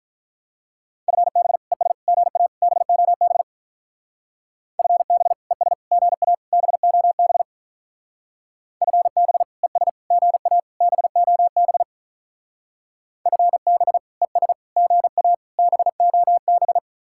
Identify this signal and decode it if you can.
Morse code, heard as a keyed tone, about 47 wpm, 700 Hz. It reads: FB ES GA BOB FB ES GA BOB FB ES GA BOB FB ES GA BOB